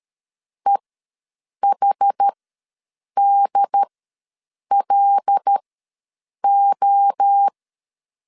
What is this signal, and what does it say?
Morse code, heard as a keyed tone, about 13 wpm, spaced as for 7 wpm, 800 Hz. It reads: EHDLO
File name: LEZIONI / 5 LBVC / 5 L ws 12wpm 2958